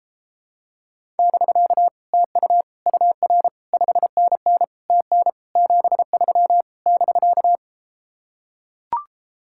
Morse code, heard as a keyed tone, about 33 wpm, 700 Hz. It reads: <BK> TU UR 5NN TN 73 <BK>